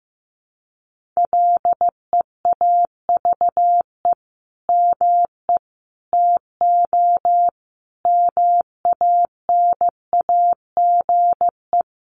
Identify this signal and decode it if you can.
Morse code, heard as a keyed tone, about 15 wpm, 700 Hz. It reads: LEAVE ME TO MANAGE
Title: Sentences from Top 1400 Words 15wpm